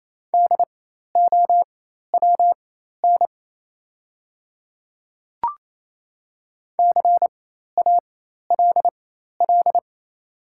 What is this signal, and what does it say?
Morse code, heard as a keyed tone, about 27 wpm, spaced as for 13 wpm, 690 Hz. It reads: DOWN CALL
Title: 4x Character Spacing - Top 100 Words - 28wpm